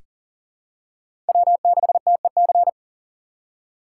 Morse code, heard as a keyed tone, about 40 wpm, 700 Hz. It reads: W6TEC
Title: USA Call Signs - 5 Characters - 3x Character Spacing - 40wpm